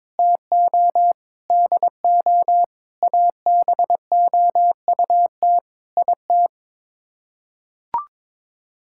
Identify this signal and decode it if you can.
Morse code, heard as a keyed tone, about 22 wpm, 700 Hz. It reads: TO DO ABOUT IT
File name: practice / 22wpm / Sets of 8 Words - 22wpm